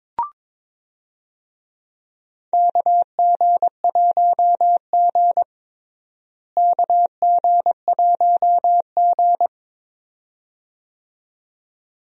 Morse code, heard as a keyed tone, about 22 wpm, 700 Hz. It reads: KG1G KG1G
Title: USA Call Signs - 4 Characters - Repeated 2x - 22wpm